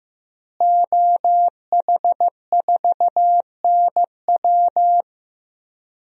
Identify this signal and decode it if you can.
Morse code, heard as a keyed tone, about 15 wpm, 700 Hz. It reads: OH4NW
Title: DXCC-100 Call Signs - 5 Characters - Repeated 3x - 15wpm